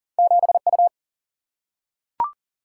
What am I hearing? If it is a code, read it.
Morse code, heard as a keyed tone, about 40 wpm, 700 Hz. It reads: 7 U
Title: Single-Letter Single-Number 40wpm